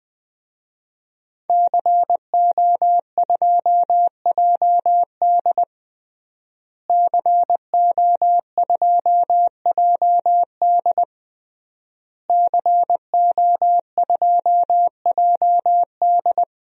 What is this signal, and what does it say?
Morse code, heard as a keyed tone, about 20 wpm, 700 Hz. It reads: CO2JD CO2JD CO2JD
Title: DXCC-100 Call Signs - 5 Characters - Repeated 3x - 20wpm